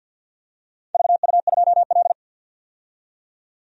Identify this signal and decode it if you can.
Morse code, heard as a keyed tone, about 51 wpm, 700 Hz. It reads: VU2L
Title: DXCC-100 Call Signs - 4 Characters - 50wpm